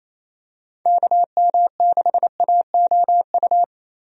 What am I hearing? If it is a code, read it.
Morse code, heard as a keyed tone, about 27 wpm, 700 Hz. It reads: KM6AOU
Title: USA Call Signs - 6 Characters - Repeated 3x - 28wpm